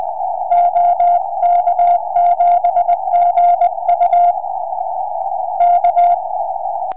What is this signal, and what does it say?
Morse code, heard as a keyed tone, about 19 wpm, 700 Hz. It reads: OK7GU K
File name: ok7gu 4m